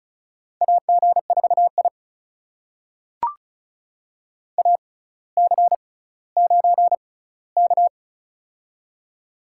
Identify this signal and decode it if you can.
Morse code, heard as a keyed tone, about 36 wpm, spaced as for 18 wpm, 700 Hz. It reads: AG4I AC9K
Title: USA Call Signs - 4 Characters - 6x Character Spacing - 35wpm